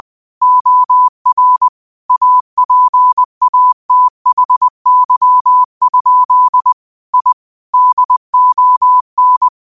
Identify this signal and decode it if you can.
Morse code, heard as a keyed tone, about 20 wpm, 1000 Hz. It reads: OR APATHY? I DON